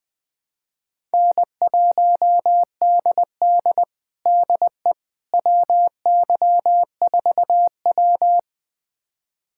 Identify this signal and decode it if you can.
Morse code, heard as a keyed tone, about 20 wpm, 700 Hz. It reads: N1DD DE WY4W
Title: USA Callsign DE Callsign - 4 Characters - 20wpm